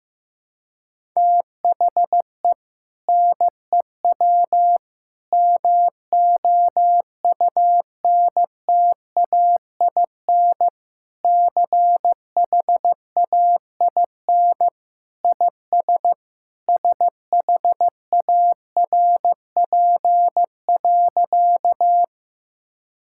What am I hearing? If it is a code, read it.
Morse code, heard as a keyed tone, about 15 wpm, 700 Hz. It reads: THE NEW MOUNTAIN CHAIN IS SHARP.
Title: Sentences from Top 1600 Words 15wpm